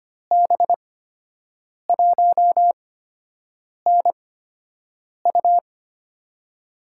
Morse code, heard as a keyed tone, about 25 wpm, spaced as for 7 wpm, 700 Hz. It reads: B1NU
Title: USA Call Signs - 5 Characters - 8x Character Spacing - 25wpm